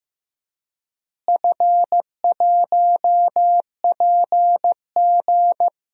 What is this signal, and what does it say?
Morse code, heard as a keyed tone, about 16 wpm, 700 Hz. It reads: F1PG
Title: DXCC-100 Call Signs - 4 Characters - Repeated 2x - 15wpm